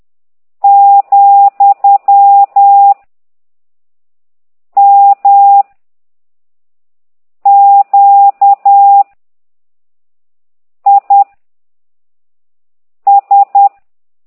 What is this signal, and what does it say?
Morse code, heard as a keyed tone, about 10 wpm, 800 Hz. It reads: , M Q I S
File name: ct2idw / exerc13